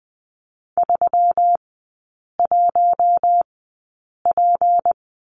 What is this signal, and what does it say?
Morse code, heard as a keyed tone, about 20 wpm, spaced as for 8 wpm, 700 Hz. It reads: 31P
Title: 152 Lesson 25e - Letter p - Three Characters 20wpm